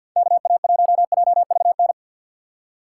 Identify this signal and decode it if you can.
Morse code, heard as a keyed tone, about 50 wpm, 700 Hz. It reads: KA1JVN
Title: USA Call Signs - 6 Characters - 3x Character Spacing - 50wpm